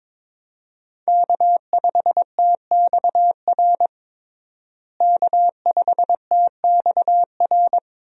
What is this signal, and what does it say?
Morse code, heard as a keyed tone, about 22 wpm, 700 Hz. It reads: K5TXR K5TXR
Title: USA Call Signs - 5 Characters - Repeated 2x - 22wpm